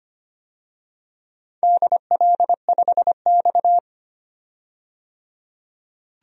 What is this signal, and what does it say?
Morse code, heard as a keyed tone, about 25 wpm, 700 Hz. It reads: DL5X
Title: DXCC-100 Call Signs - 4 Characters - 25wpm